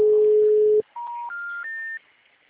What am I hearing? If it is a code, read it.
Special information tone.
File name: out- -2007- - - .50048